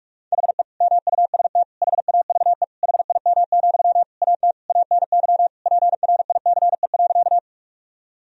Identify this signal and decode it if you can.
Morse code, heard as a keyed tone, about 46 wpm, 700 Hz. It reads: HE MUST HAVE HIM, AT ANY PRICE.
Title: Sentences from Top 1000 Words - Review Part 4 - 45wpm